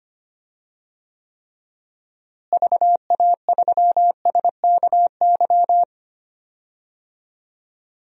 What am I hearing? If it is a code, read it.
Morse code, heard as a keyed tone, about 25 wpm, 700 Hz. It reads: VA3SKY
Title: DXCC-100 Call Signs - 6 Characters - Repeated 3x - 25wpm